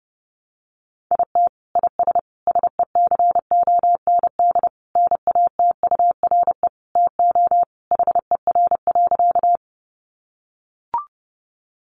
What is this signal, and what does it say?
Morse code, heard as a keyed tone, about 30 wpm, 700 Hz. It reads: IT IS SECOND NATURE TO HER.
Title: Sentences from Top 600 Words 30wpm